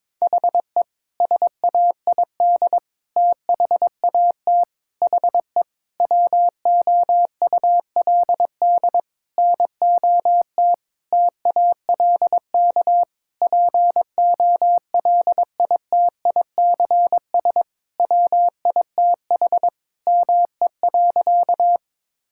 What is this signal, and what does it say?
Morse code, heard as a keyed tone, about 22 wpm, 700 Hz. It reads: HE SAID THAT HE WOULD NOT TALK POLITICS WITH ME.
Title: Sentences from Top 1000 Words - Review Part 6 - 22wpm